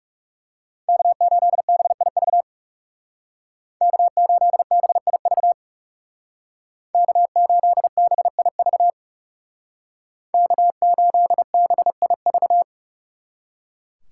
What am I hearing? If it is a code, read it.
Morse code, heard as a keyed tone, about 39 wpm, 700 Hz. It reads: K8BIV K8BIV K8BIV K8BIV